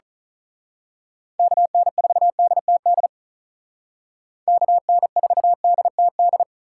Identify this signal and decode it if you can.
Morse code, heard as a keyed tone, about 37 wpm, 700 Hz. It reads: KN4DTD KN4DTD
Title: Speed Racing - USA 2x3 Call Signs 35wpm